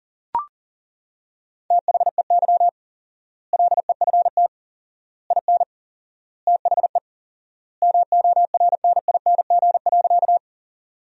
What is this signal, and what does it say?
Morse code, heard as a keyed tone, about 40 wpm, 700 Hz. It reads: THEY LEFT IN THE MORNING.
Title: 4x Word-Spacing - Sentences from Top 400 Words - 40wpm